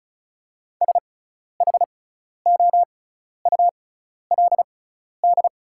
Morse code, heard as a keyed tone, about 36 wpm, spaced as for 12 wpm, 700 Hz. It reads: SHOULD
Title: 6x Character Spacing - Top 300 Words - 35wpm